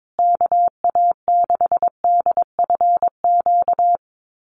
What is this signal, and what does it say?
Morse code, heard as a keyed tone, about 22 wpm, 700 Hz. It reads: KA6DFQ
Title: USA Call Signs Repeated 2x 22wpm